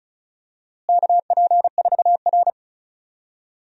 Morse code, heard as a keyed tone, about 34 wpm, 700 Hz. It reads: KP4R
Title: DXCC-100 Call Signs - 4 Characters - Repeated 3x - 35wpm